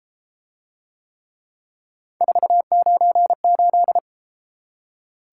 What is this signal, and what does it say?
Morse code, heard as a keyed tone, about 31 wpm, 700 Hz. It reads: 498